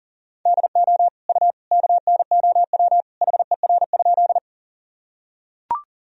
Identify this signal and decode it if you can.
Morse code, heard as a keyed tone, about 40 wpm, 700 Hz. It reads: DO U KNOW HER?
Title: Sentences from Top 100 Words and Abbreviations 40wpm